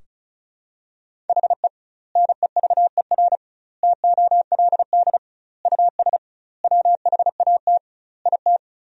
Morse code, heard as a keyed tone, about 34 wpm, 700 Hz. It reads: HE NEVER TOLD US WHAT IT
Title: 2x Word-Spacing - Sentences from Top 400 Words - 35wpm